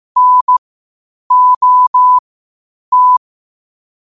Morse code, heard as a keyed tone, about 15 wpm, spaced as for 8 wpm, 1000 Hz. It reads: NOT
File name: qotdfiles / 8wpm